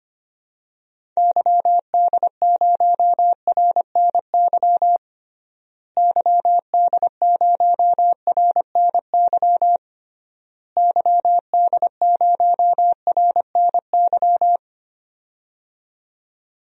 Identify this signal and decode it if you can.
Morse code, heard as a keyed tone, about 25 wpm, 700 Hz. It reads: YD0RNY YD0RNY YD0RNY
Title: DXCC-100 Call Signs - 6 Characters - Repeated 3x - 25wpm